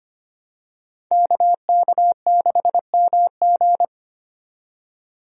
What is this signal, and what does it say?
Morse code, heard as a keyed tone, about 25 wpm, 700 Hz. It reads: KK6MG
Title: USA Call Signs - 5 Characters - 25wpm